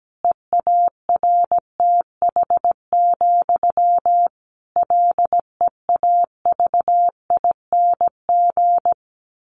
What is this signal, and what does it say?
Morse code, heard as a keyed tone, about 17 wpm, 700 Hz. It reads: EARTH, LEAVING
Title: A Sound Of Thunder 17wpm